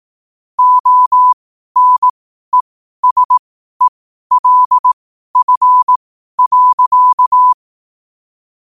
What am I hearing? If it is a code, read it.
Morse code, heard as a keyed tone, about 18 wpm, spaced as for 12 wpm, 1000 Hz. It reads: ONESELF.